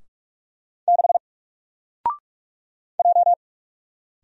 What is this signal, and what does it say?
Morse code, heard as a keyed tone, about 44 wpm, 700 Hz. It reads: 6 J
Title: Single Letter-Number 45wpm